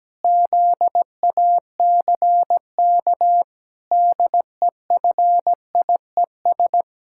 Morse code, heard as a keyed tone, about 17 wpm, 700 Hz. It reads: ZACK DEFIES
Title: Pangram Challenge v2 - 17wpm